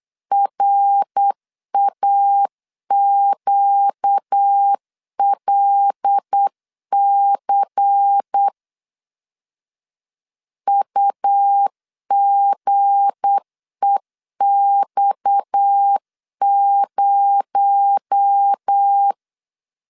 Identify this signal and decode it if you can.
Morse code, heard as a keyed tone, about 8 wpm, 800 Hz. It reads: RAQLC UGEX0
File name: LEZIONI / 7 GF9Z / 7 G 8wpm 3118